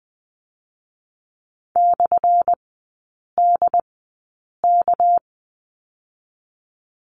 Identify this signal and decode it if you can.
Morse code, heard as a keyed tone, about 20 wpm, spaced as for 8 wpm, 700 Hz. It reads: /DK
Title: 218 Lesson 35e - Number 0 - Three Characters 20wpm